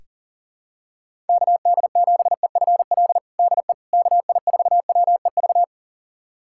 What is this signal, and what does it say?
Morse code, heard as a keyed tone, about 40 wpm, 690 Hz. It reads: KD7EFL DE KI4WEV
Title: USA Callsign DE Callsign - 6 Characters - 40wpm